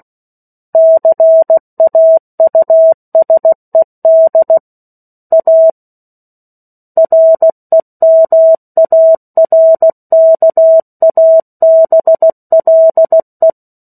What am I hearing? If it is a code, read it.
Morse code, heard as a keyed tone, about 16 wpm, 650 Hz. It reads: CAUSED A REMARKABLE
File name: 03-Chapter-03 16 WPM